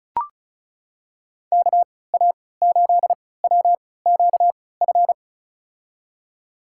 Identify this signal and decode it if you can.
Morse code, heard as a keyed tone, about 36 wpm, spaced as for 20 wpm, 700 Hz. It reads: KA8WQF
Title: USA Call Signs - 6 Characters - 3x Character Spacing - 35wpm